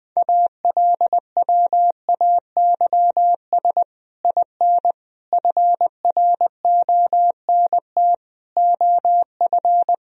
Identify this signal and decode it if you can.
Morse code, heard as a keyed tone, about 21 wpm, 700 Hz. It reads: ALWAYS IN FRONT OF
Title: Sentences from Top 1000 Words - Review Part 2 - 20wpm